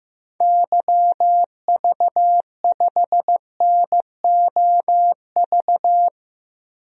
Morse code, heard as a keyed tone, about 15 wpm, 700 Hz. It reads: YV5NOV
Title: DXCC-100 Call Signs - 6 Characters - 15wpm